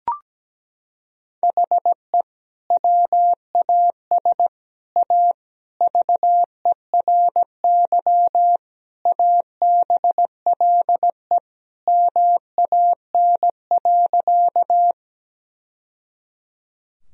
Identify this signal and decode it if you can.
Morse code, heard as a keyed tone, about 17 wpm, 700 Hz. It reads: HE WAS A VERY ABLE MAN.
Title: Sentences from Top 1000 Words - Review Part 2 - 17wpm